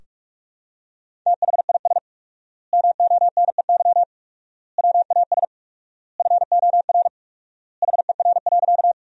Morse code, heard as a keyed tone, about 41 wpm, 700 Hz. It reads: THIS MONEY WAS FOR HER.
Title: 4x Word-Spacing - Sentences from Top 400 Words - 45wpm